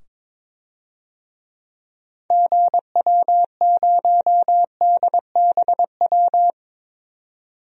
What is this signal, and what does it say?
Morse code, heard as a keyed tone, about 22 wpm, 700 Hz. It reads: GW0DBW
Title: DXCC-100 Call Signs - 6 Characters - 22wpm